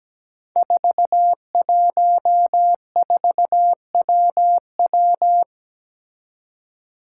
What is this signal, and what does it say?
Morse code, heard as a keyed tone, about 17 wpm, 700 Hz. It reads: 414WW
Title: Five Challenging Letter-Number Combinations 17wpm